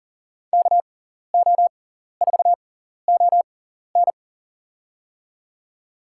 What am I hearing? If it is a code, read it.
Morse code, heard as a keyed tone, about 40 wpm, spaced as for 14 wpm, 700 Hz. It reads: KO4ON
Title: USA Call Signs - 6x Character Spacing - 40wpm